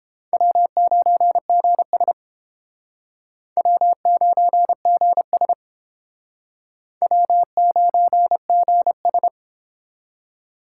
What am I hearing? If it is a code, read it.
Morse code, heard as a keyed tone, about 31 wpm, 700 Hz. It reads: W9GS W9GS W9GS